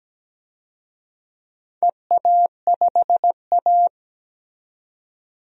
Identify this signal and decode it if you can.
Morse code, heard as a keyed tone, about 17 wpm, 700 Hz. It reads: EA5A
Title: DXCC-100 Call Signs - 4 Characters - 17wpm